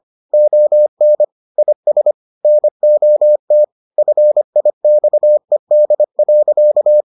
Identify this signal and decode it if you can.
Morse code, heard as a keyed tone, about 25 wpm, 600 Hz. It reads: ON IS NOT FIXED.